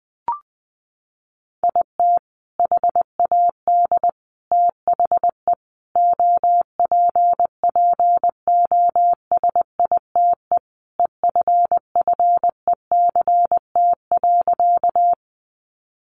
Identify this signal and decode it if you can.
Morse code, heard as a keyed tone, about 20 wpm, 700 Hz. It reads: IT HAD THE OPPOSITE EFFECT.
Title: Sentences from Top 1300 Words 20wpm